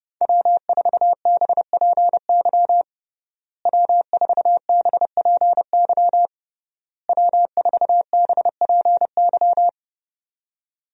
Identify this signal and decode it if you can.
Morse code, heard as a keyed tone, about 30 wpm, 700 Hz. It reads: W4BPY W4BPY W4BPY
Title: USA Call Signs - 5 Characters - Repeated 3x - 30wpm